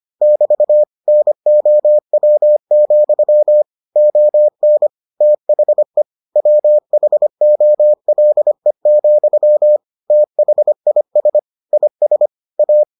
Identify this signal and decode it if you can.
Morse code, heard as a keyed tone, about 25 wpm, 600 Hz. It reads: <BT> NOW, ON THE WHOLE, THIS IS A